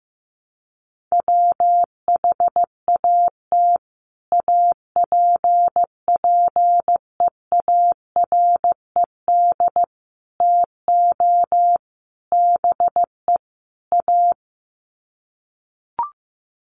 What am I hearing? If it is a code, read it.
Morse code, heard as a keyed tone, about 15 wpm, 700 Hz. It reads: WHAT APPEARED TO BE A
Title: Sets of 5 Words Encore2 15wpm